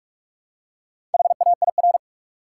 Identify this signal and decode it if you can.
Morse code, heard as a keyed tone, about 43 wpm, 700 Hz. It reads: HAIR
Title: Four Letter Words - 45wpm